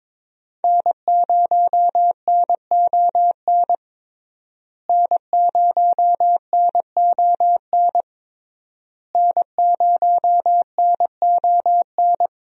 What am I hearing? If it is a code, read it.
Morse code, heard as a keyed tone, about 22 wpm, 700 Hz. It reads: N0NON N0NON N0NON
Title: USA Call Signs - 5 Characters - Repeated 3x - 22wpm